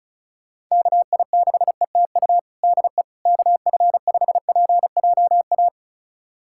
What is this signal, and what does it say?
Morse code, heard as a keyed tone, about 36 wpm, 700 Hz. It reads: KI6ETU DE KF5PJA
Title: USA Callsign DE Callsign - 6 Characters - 35wpm